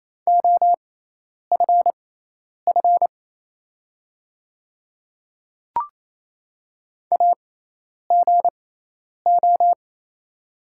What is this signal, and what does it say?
Morse code, heard as a keyed tone, about 27 wpm, spaced as for 10 wpm, 690 Hz. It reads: OFF AGO
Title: 6x Character Spacing - Top 500 Words - 28wpm